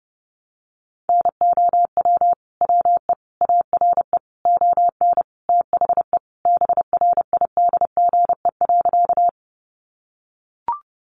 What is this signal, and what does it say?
Morse code, heard as a keyed tone, about 30 wpm, 700 Hz. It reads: NOW WE ARE ON THE BRIDGE.
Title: Sentences from Top 1200 Words 30wpm